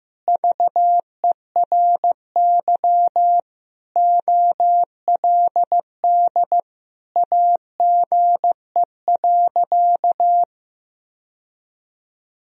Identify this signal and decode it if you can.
Morse code, heard as a keyed tone, about 15 wpm, 700 Hz. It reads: VERY OLD AGE.
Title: Sentences from Top 1000 Words - Review Part 3 - 15wpm